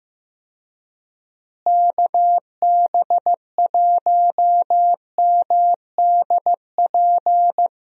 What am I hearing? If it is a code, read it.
Morse code, heard as a keyed tone, about 15 wpm, 700 Hz. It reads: KB1MDP
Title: USA Call Signs - 6 Characters - Repeated 2x - 15wpm